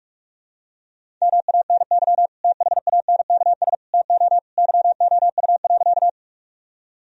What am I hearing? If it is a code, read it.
Morse code, heard as a keyed tone, about 46 wpm, 700 Hz. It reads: MANY THANKS TO YOU.
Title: Sentences from Top 1000 Words - Review Part 3 - 45wpm